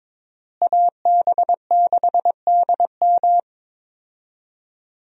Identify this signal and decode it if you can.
Morse code, heard as a keyed tone, about 22 wpm, 700 Hz. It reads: AB6DM
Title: USA Call Signs - 5 Characters - Repeated 2x - 22wpm